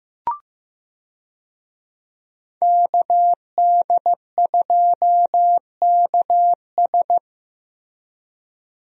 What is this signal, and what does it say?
Morse code, heard as a keyed tone, about 15 wpm, 700 Hz. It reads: KD2KS
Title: USA Call Signs - 5 Characters - Repeated 3x - 15wpm